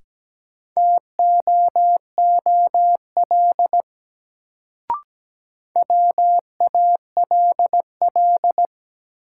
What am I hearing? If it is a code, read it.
Morse code, heard as a keyed tone, about 17 wpm, 700 Hz. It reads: TOOL WALL
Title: Sentences from Top 1000 Words - Review Part 6 - New Word Drill - 17wpm